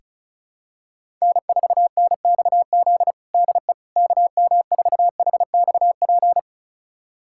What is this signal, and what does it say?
Morse code, heard as a keyed tone, about 36 wpm, 700 Hz. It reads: N4NXZ DE KM4HXP
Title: USA Callsign DE Callsign 35wpm